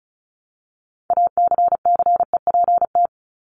Morse code, heard as a keyed tone, about 34 wpm, 700 Hz. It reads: ACCEPT
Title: ED Suffix - Top 100 Words with Base Words 35wpm